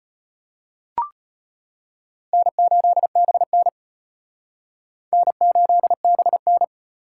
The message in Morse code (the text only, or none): N8BN N8BN